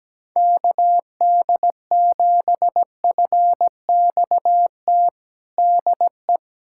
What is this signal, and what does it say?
Morse code, heard as a keyed tone, about 17 wpm, 700 Hz. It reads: KD7FXT DE
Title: USA Callsign DE Callsign - 6 Characters - 17wpm